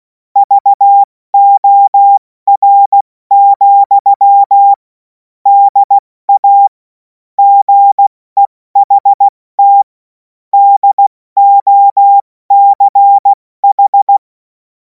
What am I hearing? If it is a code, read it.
Morse code, heard as a keyed tone, about 16 wpm, 800 Hz. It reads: VOR, DA GEHT DOCH
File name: Morsekurs / Witze / Witz 11. August 2026 70 F80